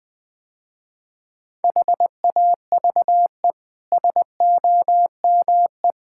Morse code, heard as a keyed tone, about 20 wpm, 700 Hz. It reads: HAVE SOME
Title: Easy Sentences with Hints - 20wpm